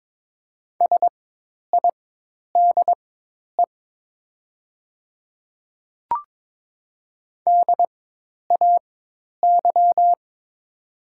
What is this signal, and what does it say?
Morse code, heard as a keyed tone, about 22 wpm, spaced as for 10 wpm, 700 Hz. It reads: SIDE DAY
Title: 4x Character Spacing - Top 100 Words - 22wpm